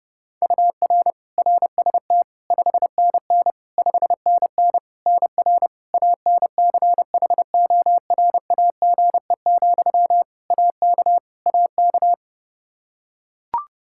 Morse code, heard as a keyed tone, about 30 wpm, 700 Hz. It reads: UR RST 5NN 5NN NR ANCHORAGE, AK AK
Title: Practice QSO - Exchange 1 - Beginning - 30wpm